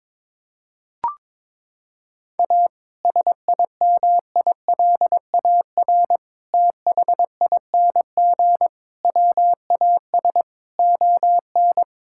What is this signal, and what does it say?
Morse code, heard as a keyed tone, about 22 wpm, 700 Hz. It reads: A SIMILAR THING WAS ON
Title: Sentences from Top 1000 Words - Review Part 5 - 22wpm